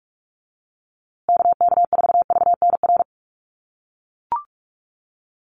Morse code, heard as a keyed tone, about 44 wpm, 700 Hz. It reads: KK4VNR
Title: USA Call Signs Repeated 2x 45wpm